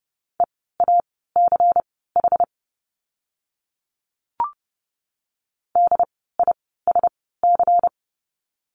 Morse code, heard as a keyed tone, about 30 wpm, spaced as for 17 wpm, 700 Hz. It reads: EACH DISC
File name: Lesson 16f - Letter c - Words 30wpm